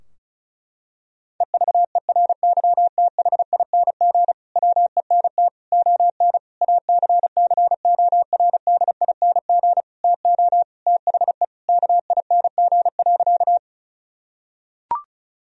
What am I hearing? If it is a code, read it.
Morse code, heard as a keyed tone, about 36 wpm, 700 Hz. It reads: EVERYTHING WENT ON ACCORDING TO THE KING.
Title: Sentences from Top 1000 Words - Review Part 4 - 35wpm